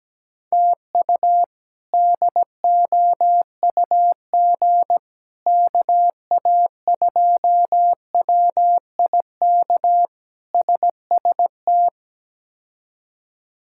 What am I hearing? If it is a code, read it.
Morse code, heard as a keyed tone, about 17 wpm, 700 Hz. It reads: TU DOUG KA2WIK SST
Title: K1USN SST QSO Practice - 17wpm